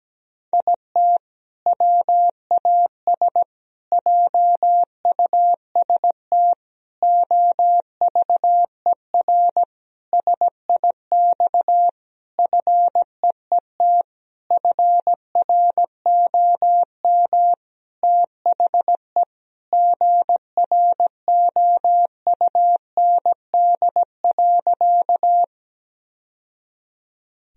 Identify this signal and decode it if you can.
Morse code, heard as a keyed tone, about 17 wpm, 700 Hz. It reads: IT WAS JUST OVER SIX FEET FROM THE GROUND.
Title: Sentences from Top 1000 Words - Review Part 2 - 17wpm